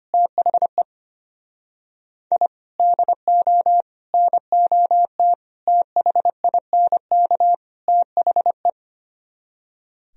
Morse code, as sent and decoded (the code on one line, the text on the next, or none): - .... / .. / -.. --- / -. --- - / - .... .. -. -.- / - .... .
THE I DO NOT THINK THE